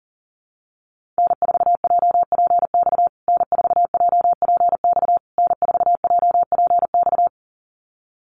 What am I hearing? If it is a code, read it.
Morse code, heard as a keyed tone, about 40 wpm, 700 Hz. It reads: N4JPX N4JPX N4JPX
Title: USA Call Signs Repeated 3x 40wpm